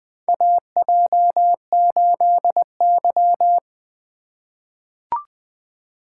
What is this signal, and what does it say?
Morse code, heard as a keyed tone, about 21 wpm, 700 Hz. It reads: AJ8Y
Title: USA Call Signs - 4 Characters - 3x Character Spacing - 20wpm